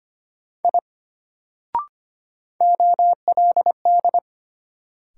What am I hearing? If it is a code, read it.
Morse code, heard as a keyed tone, about 25 wpm, 700 Hz. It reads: I OLD
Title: Sentences from Top 200 Words - New Word Drill - 25wpm